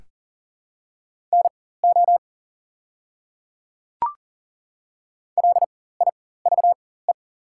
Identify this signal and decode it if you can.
Morse code, heard as a keyed tone, about 40 wpm, spaced as for 19 wpm, 700 Hz. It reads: NO LIVE
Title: 4x Character Spacing - Top 200 Words - 40wpm